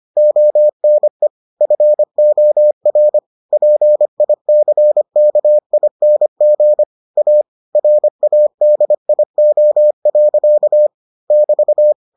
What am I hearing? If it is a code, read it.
Morse code, heard as a keyed tone, about 25 wpm, 600 Hz. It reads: ONE FOR PICKING A RADIO. <BT>